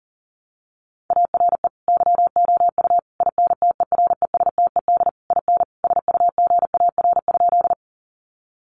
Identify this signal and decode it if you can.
Morse code, heard as a keyed tone, about 40 wpm, 700 Hz. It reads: ARE YOU INTERESTED IN SUGAR?
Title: Sentences from Top 1200 Words 40wpm